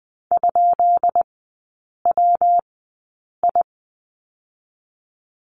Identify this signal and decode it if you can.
Morse code, heard as a keyed tone, about 20 wpm, spaced as for 8 wpm, 690 Hz. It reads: ?WI
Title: 159 Lesson 26e - Letter g - Three Characters 20wpm